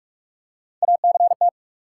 Morse code, heard as a keyed tone, about 46 wpm, 700 Hz. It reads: ACT
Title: Three Letter Words - 45wpm